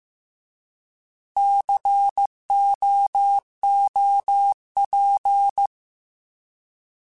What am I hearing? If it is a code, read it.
Morse code, heard as a keyed tone, about 15 wpm, 800 Hz. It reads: COOP
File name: Alphabetical / Alphabetical mp3s / Cs @13